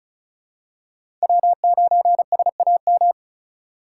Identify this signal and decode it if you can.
Morse code, heard as a keyed tone, about 34 wpm, 700 Hz. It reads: W9SAM
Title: USA Call Signs - 5 Characters - 3x Character Spacing - 35wpm